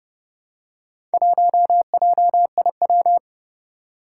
Morse code, heard as a keyed tone, about 30 wpm, 700 Hz. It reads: 1JIW
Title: Four Challenging Letter-Number Combinations 30wpm